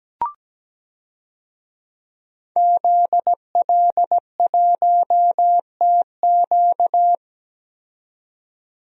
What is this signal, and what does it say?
Morse code, heard as a keyed tone, about 17 wpm, 700 Hz. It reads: ZL1TQ
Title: DXCC-100 Call Signs - 5 Characters - 17wpm